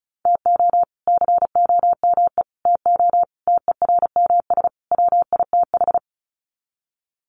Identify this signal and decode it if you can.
Morse code, heard as a keyed tone, about 34 wpm, 700 Hz. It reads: TO COME TO TERMS WITH
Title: Sets of 5 Words Encore 35wpm